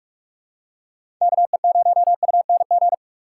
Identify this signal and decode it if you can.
Morse code, heard as a keyed tone, about 45 wpm, 700 Hz. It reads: KE0UNG
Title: USA Call Signs - 6 Characters - 3x Character Spacing - 45wpm